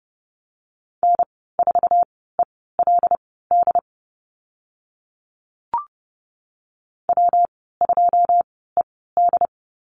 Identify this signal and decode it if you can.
Morse code, heard as a keyed tone, about 30 wpm, spaced as for 17 wpm, 700 Hz. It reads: N4ELD W2ED
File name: Lesson 19g - Letter w - Call Signs 30wpm